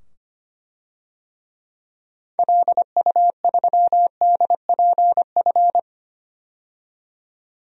Morse code, heard as a keyed tone, about 25 wpm, 700 Hz. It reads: LU3DPF
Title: DXCC-100 Call Signs - 6 Characters - 25wpm